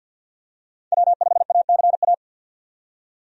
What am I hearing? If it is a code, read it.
Morse code, heard as a keyed tone, about 51 wpm, 700 Hz. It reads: W5ACA
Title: USA Call Signs - 5 Characters - 50wpm